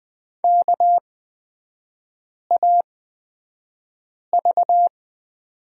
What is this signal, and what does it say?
Morse code, heard as a keyed tone, about 20 wpm, spaced as for 5 wpm, 700 Hz. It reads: KAV
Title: 224 Lesson 36e - BT - Three Characters 20wpm